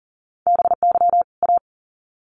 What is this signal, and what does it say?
Morse code, heard as a keyed tone, about 40 wpm, 700 Hz. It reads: BY A
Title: Sets of 2 Words 40wpm